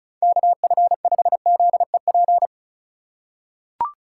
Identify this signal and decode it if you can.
Morse code, heard as a keyed tone, about 36 wpm, 700 Hz. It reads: KF5ZEP
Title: USA Call Signs - 6 Characters - 2x Character Spacing - 35wpm